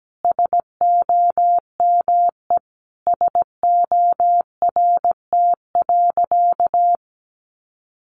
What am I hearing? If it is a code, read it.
Morse code, heard as a keyed tone, about 17 wpm, 700 Hz. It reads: SOME SORT.
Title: Sentences from Top 1300 Words 17wpm